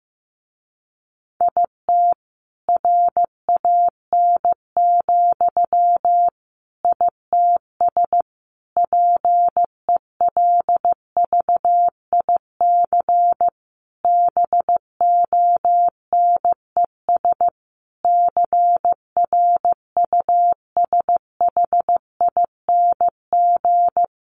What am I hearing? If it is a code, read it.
Morse code, heard as a keyed tone, about 15 wpm, 700 Hz. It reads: IT RAN, ITS PELVIC BONES CRUSHING
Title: A Sound Of Thunder 15wpm